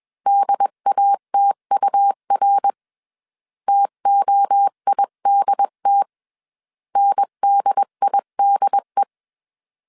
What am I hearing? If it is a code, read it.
Morse code, heard as a keyed tone, about 21 wpm, 800 Hz. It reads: DATUR TOIDT NDIDE
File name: 20wpm